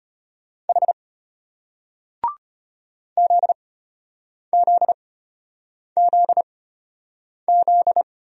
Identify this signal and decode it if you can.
Morse code, heard as a keyed tone, about 34 wpm, spaced as for 8 wpm, 700 Hz. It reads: H ZZZZ